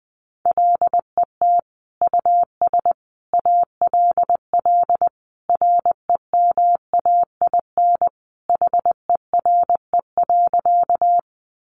Morse code, heard as a keyed tone, about 20 wpm, 700 Hz. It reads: LET US ALL REMAIN HERE.
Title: Sentences from Top 1200 Words 20wpm